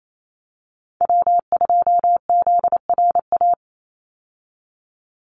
Morse code, heard as a keyed tone, about 28 wpm, 700 Hz. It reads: W2ZRA
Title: Call Signs Repeated 2x - 28wpm